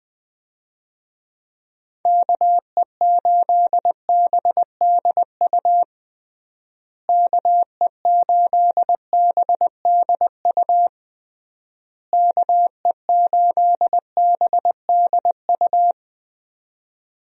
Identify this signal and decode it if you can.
Morse code, heard as a keyed tone, about 20 wpm, 700 Hz. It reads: KE8BDU KE8BDU KE8BDU
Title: USA Call Signs - 6 Characters - Repeated 3x - 20wpm